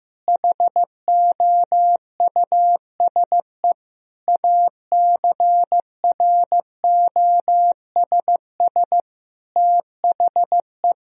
Morse code, heard as a keyed tone, about 15 wpm, 700 Hz. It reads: HOUSE ACROSS THE
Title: Sentences from Top 1000 Words - Review Part 1 - 15wpm